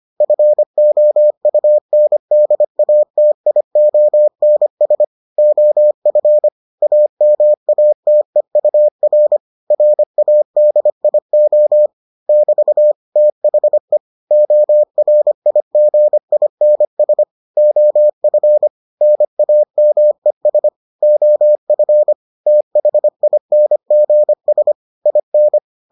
Morse code, heard as a keyed tone, about 25 wpm, 600 Hz. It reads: FOUNDATIONS OF AMATEUR RADIO <BT> THE ORIGINS OF NAMES OF THINGS IN